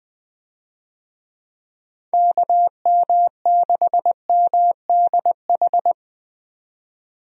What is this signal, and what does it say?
Morse code, heard as a keyed tone, about 20 wpm, 700 Hz. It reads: KM6MDH
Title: USA Call Signs - 6 Characters - 20wpm